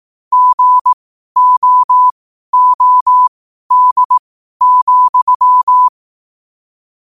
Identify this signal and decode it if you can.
Morse code, heard as a keyed tone, about 18 wpm, spaced as for 12 wpm, 1000 Hz. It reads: GOOD,
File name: qotdfiles / 12wpm